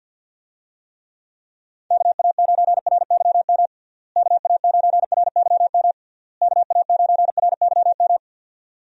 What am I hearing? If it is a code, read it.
Morse code, heard as a keyed tone, about 51 wpm, 700 Hz. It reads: KA9RYM KA9RYM KA9RYM
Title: USA Call Signs - 6 Characters - Repeated 3x - 50wpm